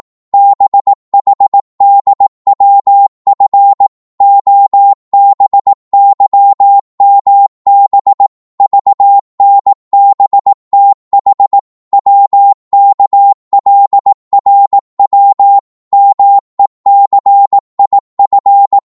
Morse code, heard as a keyed tone, about 18 wpm, 800 Hz. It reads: BHDWF OBYMB VNBTH WKLRW MECIF
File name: Morse / ltr-18wpm